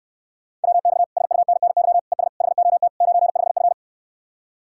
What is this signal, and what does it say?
Morse code, heard as a keyed tone, about 69 wpm, 700 Hz. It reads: WX SUNNY ES HOT 95F